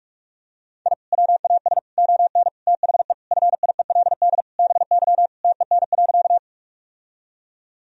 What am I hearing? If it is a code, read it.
Morse code, heard as a keyed tone, about 44 wpm, 700 Hz. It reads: I WAS ON THE FIELD BY TEN.
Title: Sentences from Top 1000 Words - Review Part 2 - 45wpm